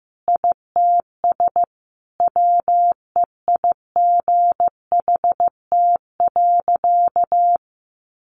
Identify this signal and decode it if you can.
Morse code, heard as a keyed tone, about 15 wpm, 700 Hz. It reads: ITS WEIGHT.
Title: A Sound Of Thunder 15wpm